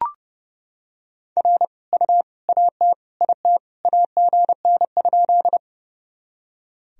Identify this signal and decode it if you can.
Morse code, heard as a keyed tone, about 30 wpm, 700 Hz. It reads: R U AT IT AGN?
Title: Sentences from Top 200 Words and Abbreviations 30wpm